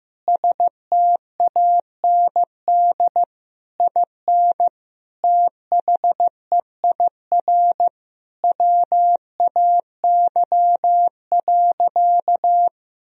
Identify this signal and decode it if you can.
Morse code, heard as a keyed tone, about 15 wpm, 700 Hz. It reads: STAND IN THEIR WAY.
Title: Sentences from Top 1000 Words - Review Part 1 - 15wpm